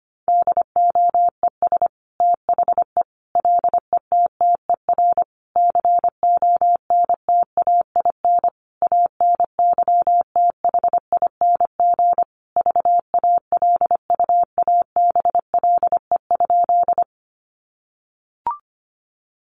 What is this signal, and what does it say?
Morse code, heard as a keyed tone, about 25 wpm, 700 Hz. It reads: DOES THE LETTER CONTAIN ANYTHING VALUABLE?
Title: Sentences from Top 1100 Words 25wpm